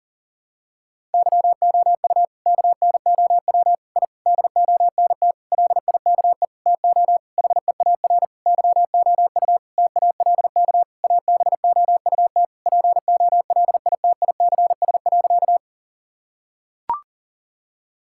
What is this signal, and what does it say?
Morse code, heard as a keyed tone, about 40 wpm, 700 Hz. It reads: YOU KNOW I DONT LIKE TO HEAR YOU TALK ABOUT POLITICS.
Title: Sentences from Top 1000 Words - Review Part 6 - 40wpm